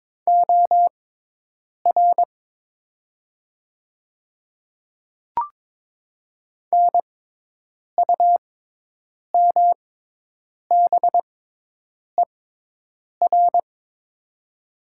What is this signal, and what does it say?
Morse code, heard as a keyed tone, about 22 wpm, spaced as for 8 wpm, 700 Hz. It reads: OR NUMBER
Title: 6x Character Spacing - Top 100 Words - 22wpm